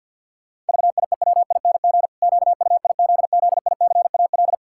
Morse code, heard as a keyed tone, about 50 wpm, 700 Hz. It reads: VIEWING QUIZZICAL
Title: Pangram Challenge v2 - 50wpm